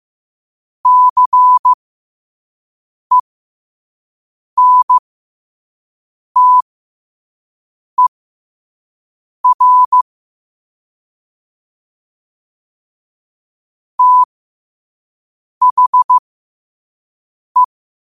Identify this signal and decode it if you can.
Morse code, heard as a keyed tone, about 15 wpm, spaced as for 5 wpm, 1000 Hz. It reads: CENTER THE
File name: 5wpm